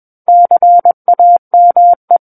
Morse code, heard as a keyed tone, about 21 wpm, 700 Hz. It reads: CAME
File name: Book One / 12-Chapter 12 - 21 WPM